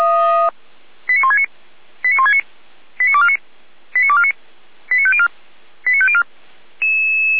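Five-tone selective call (ZVEI), heard as quick runs of five tones, tones of about 70 ms. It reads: 89169 89169 89249 89249 89504 89504
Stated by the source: ZVEI five-tone selective call, code 89171, received via SDR.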